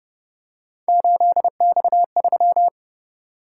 Morse code, heard as a keyed tone, about 28 wpm, 700 Hz. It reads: 8X3